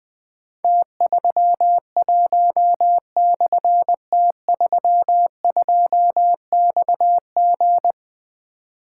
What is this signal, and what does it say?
Morse code, heard as a keyed tone, about 20 wpm, 700 Hz. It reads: T31/T32XG
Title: Difficult Call Signs 20wpm